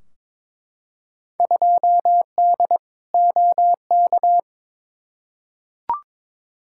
Morse code, heard as a keyed tone, about 21 wpm, 700 Hz. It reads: 2D OK
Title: Field Day Exchanges Encore - 22wpm